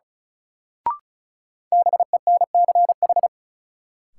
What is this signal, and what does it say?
Morse code, heard as a keyed tone, about 36 wpm, 700 Hz. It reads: BENCH
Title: Sentences from Top 1400 Words - New Word Drill - 35wpm